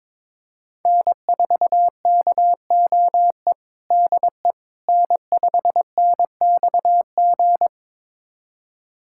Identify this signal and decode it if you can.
Morse code, heard as a keyed tone, about 22 wpm, 700 Hz. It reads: N4KOE DE N5NXG
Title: USA Callsign DE Callsign - 5 Characters - 22wpm